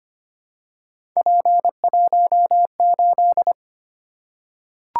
Morse code, heard as a keyed tone, about 25 wpm, 700 Hz. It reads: P18
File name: Three Letter-Number Combinations - 25wpm